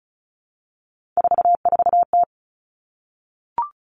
Morse code, heard as a keyed tone, about 36 wpm, 700 Hz. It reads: 44T
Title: Three Challenging Letter-Number Combinations 35wpm